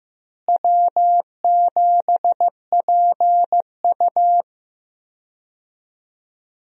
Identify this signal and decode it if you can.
Morse code, heard as a keyed tone, about 15 wpm, 700 Hz. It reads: W7PU
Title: USA Call Signs - 4 Characters - Repeated 2x - 15wpm